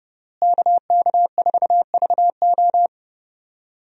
Morse code, heard as a keyed tone, about 30 wpm, 700 Hz. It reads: KK4VO
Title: USA Call Signs - 5 Characters - Repeated 2x - 30wpm